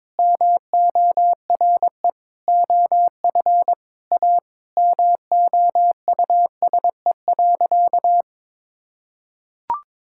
Morse code, heard as a keyed tone, about 21 wpm, 700 Hz. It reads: MORE OF A MOUSE.
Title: Sentences from Top 1000 Words - Review Part 5 - 22wpm